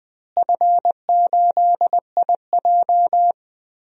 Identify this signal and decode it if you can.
Morse code, heard as a keyed tone, about 21 wpm, 700 Hz. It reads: F8IJ
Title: DXCC-100 Call Signs - 4 Characters - Repeated 3x - 20wpm